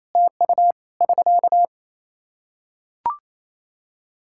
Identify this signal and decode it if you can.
Morse code, heard as a keyed tone, about 28 wpm, 700 Hz. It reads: TU <SK>
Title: Practice QSO - Exchange 4 - Ending - 28wpm